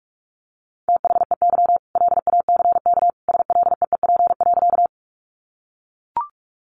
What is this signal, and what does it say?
Morse code, heard as a keyed tone, about 44 wpm, 700 Hz. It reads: THEY LACK SLEEP.
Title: Sentences from Top 1100 Words 45wpm